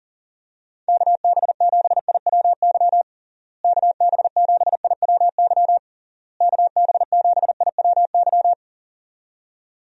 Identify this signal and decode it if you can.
Morse code, heard as a keyed tone, about 40 wpm, 700 Hz. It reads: KB7IWY KB7IWY KB7IWY
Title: USA Call Signs - 6 Characters - Repeated 3x - 40wpm